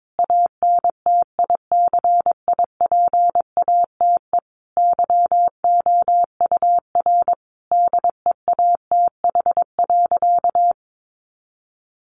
Morse code, heard as a keyed tone, about 22 wpm, 700 Hz. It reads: ANTICIPATE YOUR DEATH.
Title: Sentences from Top 1500 Words 22wpm